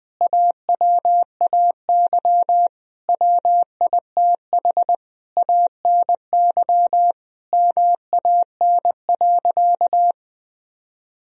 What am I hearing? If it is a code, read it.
Morse code, heard as a keyed tone, about 21 wpm, 700 Hz. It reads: AWAY WITH ANY MAN.
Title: Sentences from Top 1000 Words - Review Part 1 - 20wpm